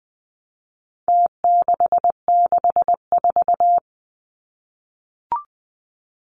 Morse code, heard as a keyed tone, about 20 wpm, 700 Hz. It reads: T664
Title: Numbers - Contest Style - 4-Digit 20wpm